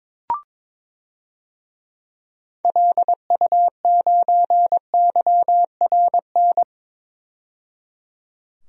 Morse code, heard as a keyed tone, about 22 wpm, 700 Hz. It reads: LU9YRN
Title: DXCC-100 Call Signs - 6 Characters - 22wpm